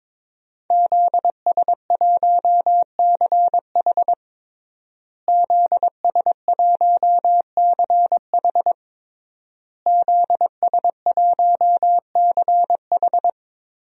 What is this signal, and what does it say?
Morse code, heard as a keyed tone, about 22 wpm, 700 Hz. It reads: ZS1CH ZS1CH ZS1CH
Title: DXCC-100 Call Signs - 5 Characters - Repeated 3x - 22wpm